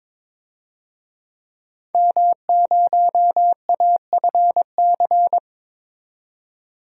Morse code, heard as a keyed tone, about 21 wpm, 700 Hz. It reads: M0AFC
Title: DXCC-100 Call Signs - 5 Characters - 22wpm